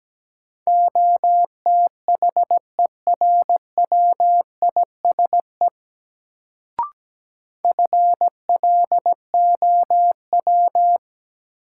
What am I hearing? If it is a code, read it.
Morse code, heard as a keyed tone, about 17 wpm, 700 Hz. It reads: OTHERWISE FLOW
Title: Sentences from Top 1200 Words - New Word Drill - 17wpm